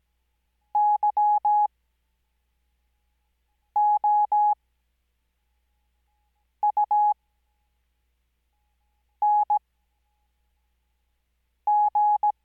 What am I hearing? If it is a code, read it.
Morse code, heard as a keyed tone, about 17 wpm, spaced as for 4 wpm, 840 Hz. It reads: YOUNG
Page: Farnsworth Spacing: 15WPM Character speed Spaced at 5WPM.